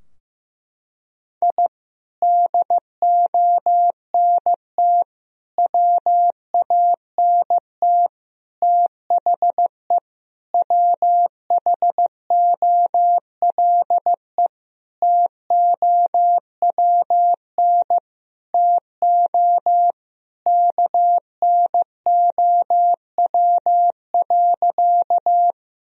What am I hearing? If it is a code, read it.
Morse code, heard as a keyed tone, about 15 wpm, 700 Hz. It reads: I DONT WANT THE WHOLE TOWN TO KNOW.
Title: Sentences from Top 1000 Words - Review Part 2 - 15wpm